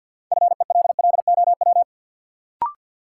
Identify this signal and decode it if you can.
Morse code, heard as a keyed tone, about 52 wpm, 700 Hz. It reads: FELLOW
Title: Sets of 5 Words Encore - New Word Drill - 50wpm